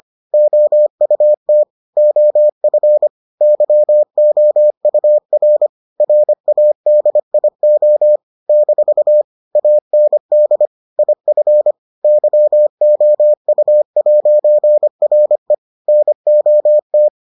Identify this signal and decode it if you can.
Morse code, heard as a keyed tone, about 25 wpm, 600 Hz. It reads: OUT OF YOUR RADIO - AND IF YOU'RE NOT